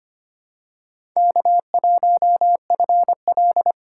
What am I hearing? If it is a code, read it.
Morse code, heard as a keyed tone, about 25 wpm, 700 Hz. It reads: K1FL
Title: USA Call Signs - 3x Character Spacing - 25wpm